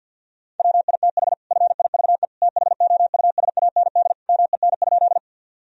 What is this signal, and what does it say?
Morse code, heard as a keyed tone, about 50 wpm, 700 Hz. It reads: WITH FIVE THOUSAND MEN?
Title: Sentences from Top 1000 Words - Review Part 3 - 50wpm